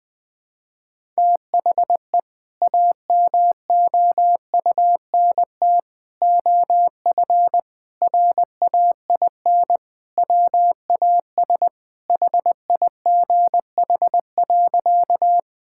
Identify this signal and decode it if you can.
Morse code, heard as a keyed tone, about 21 wpm, 700 Hz. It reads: THE AMOUNT OF RAIN WAS HIGH.
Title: Sentences with 3-Word Phrases - 20wpm